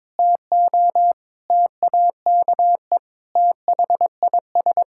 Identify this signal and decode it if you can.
Morse code, heard as a keyed tone, about 22 wpm, 700 Hz. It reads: TO TAKE THIS
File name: practice / 22wpm / Sets of 8 Words - 22wpm